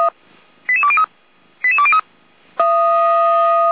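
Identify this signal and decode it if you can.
Five-tone selective call (ZVEI), heard as quick runs of five tones, tones of about 70 ms. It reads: 88203 88203
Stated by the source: ZVEI five-tone selective call received with an SDR.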